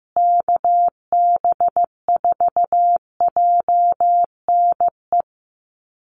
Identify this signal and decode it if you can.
Morse code, heard as a keyed tone, about 15 wpm, 700 Hz. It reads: KB4JNE
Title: USA Call Signs Repeated 3x 15wpm